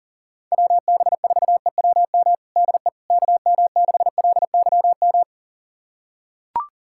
Morse code, heard as a keyed tone, about 40 wpm, 700 Hz. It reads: WB4EWM DE KM6LYM
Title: USA Callsign DE Callsign 40wpm